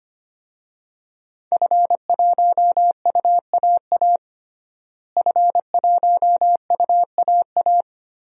Morse code, heard as a keyed tone, about 25 wpm, 700 Hz. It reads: F1UAA F1UAA
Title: DXCC-100 Call Signs - 5 Characters - Repeated 3x - 25wpm